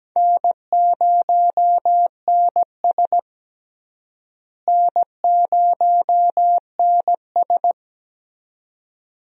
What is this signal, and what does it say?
Morse code, heard as a keyed tone, about 17 wpm, 700 Hz. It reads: N0NS N0NS
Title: USA Call Signs - 4 Characters - Repeated 2x - 17wpm